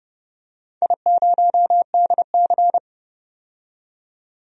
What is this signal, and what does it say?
Morse code, heard as a keyed tone, about 31 wpm, 700 Hz. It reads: I0DC
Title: DXCC-100 Call Signs - 4 Characters - 30wpm